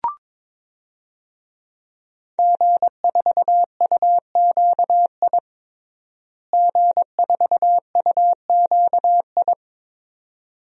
Morse code, heard as a keyed tone, about 22 wpm, 700 Hz. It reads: G4UQI G4UQI
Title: DXCC-100 Call Signs - 5 Characters - Repeated 2x - 22wpm